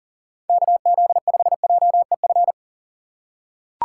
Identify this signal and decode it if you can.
Morse code, heard as a keyed tone, about 38 wpm, 700 Hz. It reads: KZ5JEF
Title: USA Call Signs - 6 Characters - 4x Character Spacing - 40wpm